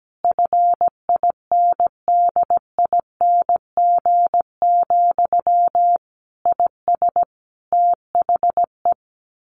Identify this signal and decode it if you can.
Morse code, heard as a keyed tone, about 17 wpm, 700 Hz. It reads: FINDING, IS THE
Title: Sentences from Top 1100 Words 17wpm